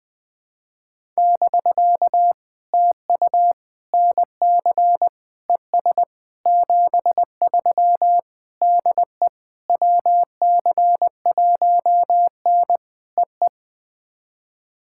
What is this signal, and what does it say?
Morse code, heard as a keyed tone, about 20 wpm, 700 Hz. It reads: <BK> TU NC ES 73 DE WC1N EE
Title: POTA QSO Practice - 20wpm